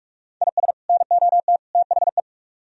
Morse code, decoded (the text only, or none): IS NOT THE